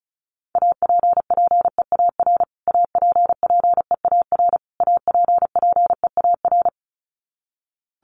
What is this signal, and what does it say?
Morse code, heard as a keyed tone, about 34 wpm, 700 Hz. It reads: APPEAR APPEAR APPEAR
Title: Top 500 Words Repeated 35wpm